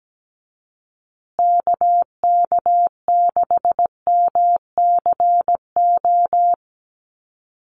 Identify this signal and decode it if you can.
Morse code, heard as a keyed tone, about 17 wpm, 700 Hz. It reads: KK6MCO
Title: USA Call Signs Repeated 3x 17wpm